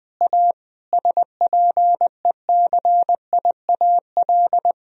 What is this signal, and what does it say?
Morse code, heard as a keyed tone, about 20 wpm, 700 Hz. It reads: A SPECIAL
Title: Sentences from Top 1000 Words - Review Part 3 - 20wpm